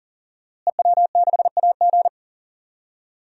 Morse code, heard as a keyed tone, about 37 wpm, 700 Hz. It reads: EW6AG
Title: DXCC-100 Call Signs - 5 Characters - Repeated 3x - 40wpm